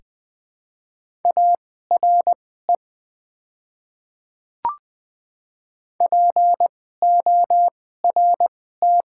Morse code, heard as a keyed tone, about 20 wpm, spaced as for 14 wpm, 700 Hz. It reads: ARE PORT
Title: 2x Character Spacing - Top 200 Words - 20wpm